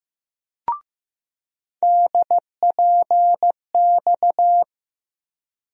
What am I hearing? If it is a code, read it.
Morse code, heard as a keyed tone, about 15 wpm, 700 Hz. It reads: DPX
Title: USA Call Sign Postfixes - 3 Letters 15wpm